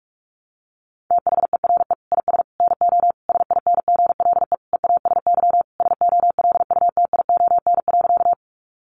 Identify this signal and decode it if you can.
Morse code, heard as a keyed tone, about 44 wpm, 700 Hz. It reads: THERE IS NO SINGLE EASY SOLUTION.
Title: Sentences from Top 1300 Words 45wpm